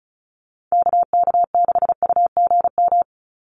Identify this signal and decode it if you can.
Morse code, heard as a keyed tone, about 34 wpm, 700 Hz. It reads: KK6UGM
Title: USA Call Signs 35wpm